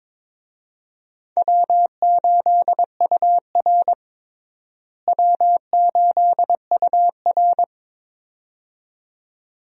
Morse code, heard as a keyed tone, about 22 wpm, 700 Hz. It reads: W8UR W8UR
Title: USA Call Signs - 4 Characters - Repeated 2x - 22wpm